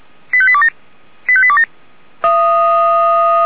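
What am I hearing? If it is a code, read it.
Five-tone selective call (ZVEI), heard as quick runs of five tones, tones of about 70 ms. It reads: 85827 85827
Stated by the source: ZVEI five-tone call received on an SDR.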